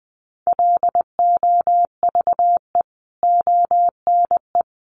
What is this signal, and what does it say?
Morse code, heard as a keyed tone, about 20 wpm, 700 Hz. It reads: LOVE ONE
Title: Sentences from Top 600 Words 20wpm